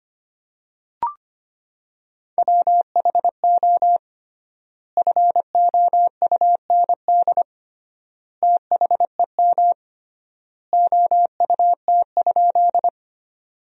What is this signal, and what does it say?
Morse code, heard as a keyed tone, about 25 wpm, 700 Hz. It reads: WHO FOUND THEM OUT?
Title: 3x Word-Spacing - Sentences from Top 300 Words - 25wpm